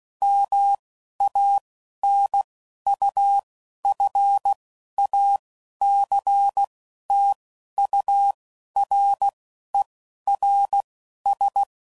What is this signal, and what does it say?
Morse code, heard as a keyed tone, about 16 wpm, spaced as for 11 wpm, 800 Hz. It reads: MANUFACTURERS
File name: Sayings / Quotes / Airtight